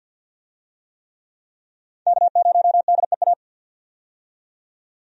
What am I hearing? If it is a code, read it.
Morse code, heard as a keyed tone, about 45 wpm, 700 Hz. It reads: K0DEA
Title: USA Call Signs - 5 Characters - 50wpm